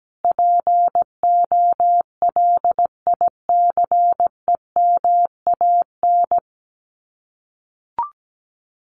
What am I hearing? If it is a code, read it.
Morse code, heard as a keyed tone, about 17 wpm, 700 Hz. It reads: POLICEMAN
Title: Common QSO Phrases Encore 17wpm